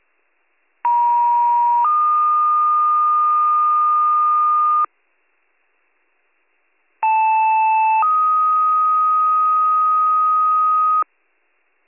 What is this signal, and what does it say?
Two-tone page, heard as a one-second tone, then a three-second tone, about 950 then 1220 Hz, about 880 then 1250 Hz. It is heard from 0.8 to 11.0 s.